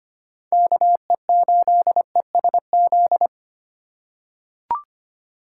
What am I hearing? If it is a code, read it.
Morse code, heard as a keyed tone, about 25 wpm, 700 Hz. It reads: KE8ESZ
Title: USA Call Signs - 6 Characters - 3x Character Spacing - 25wpm